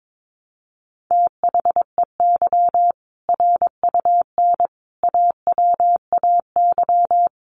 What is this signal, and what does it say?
Morse code, heard as a keyed tone, about 22 wpm, 700 Hz. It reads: THEY RUN AWAY
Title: Easy Phrases 22wpm